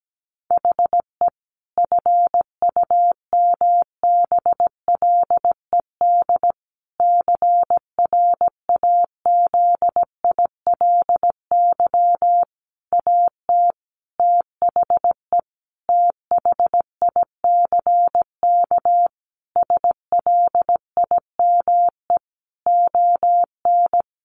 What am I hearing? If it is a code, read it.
Morse code, heard as a keyed tone, about 17 wpm, 700 Hz. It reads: HE FUMBLED CRAZILY AT THE THICK SLIME ON
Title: A Sound Of Thunder 17wpm